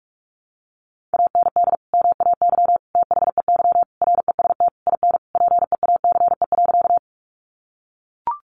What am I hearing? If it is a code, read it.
Morse code, heard as a keyed tone, about 44 wpm, 700 Hz. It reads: AND MAY THEY REST IN PEACE.
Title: Sentences from Top 1500 Words 45wpm